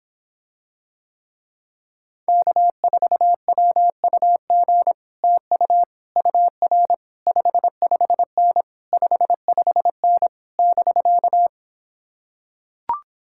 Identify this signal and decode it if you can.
Morse code, heard as a keyed tone, about 26 wpm, 700 Hz. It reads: K4WUG TU UR 55N 55N <BK>